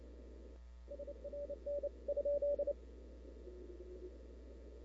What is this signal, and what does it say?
Morse code, heard as a keyed tone, about 29 wpm, 570 Hz. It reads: SRN ?
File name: CW / 2005-k5tr arrl cw dx -00:00